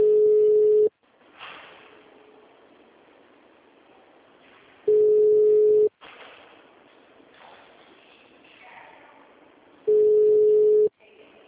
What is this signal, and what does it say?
Ringback tone (Europe, 425 Hz), with speech.